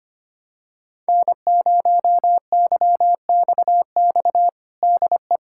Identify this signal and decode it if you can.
Morse code, heard as a keyed tone, about 25 wpm, 700 Hz. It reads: N0YXX DE
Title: USA Callsign DE Callsign - 5 Characters - 25wpm